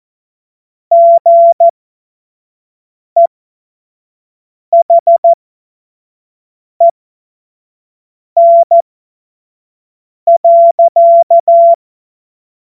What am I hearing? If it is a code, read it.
Morse code, heard as a keyed tone, about 14 wpm, spaced as for 5 wpm, 680 Hz. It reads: GEHEN.
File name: Morsekurs / Witze / Witz 09. Dezember 2025 25 F70